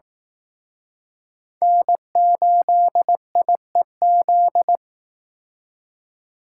Morse code, heard as a keyed tone, about 18 wpm, 700 Hz. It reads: N8IEZ
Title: Speed Racing - USA Call Signs 15wpm